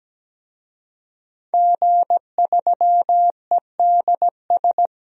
Morse code, heard as a keyed tone, about 17 wpm, 700 Hz. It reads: G3EDS
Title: DXCC-100 Call Signs - 5 Characters - 17wpm